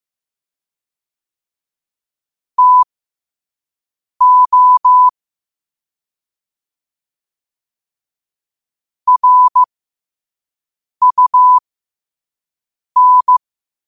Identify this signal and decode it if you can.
Morse code, heard as a keyed tone, about 15 wpm, spaced as for 5 wpm, 1000 Hz. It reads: TO RUN
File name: qotdfiles / 5wpm